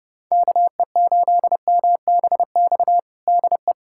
Morse code, heard as a keyed tone, about 31 wpm, 700 Hz. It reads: KE8MBX DE
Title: USA Callsign DE Callsign 30wpm